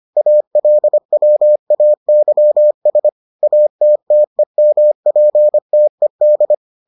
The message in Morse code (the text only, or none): ALWAYS ATTEMPTED